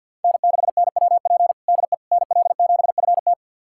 Morse code, heard as a keyed tone, about 51 wpm, 700 Hz. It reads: N6NWP DE NL7FT
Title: USA Callsign DE Callsign - 5 Characters - 50wpm